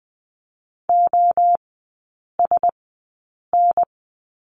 Morse code, heard as a keyed tone, about 20 wpm, spaced as for 8 wpm, 700 Hz. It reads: OSN
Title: 047 Lesson 9e - Number 4 - Three Characters 20wpm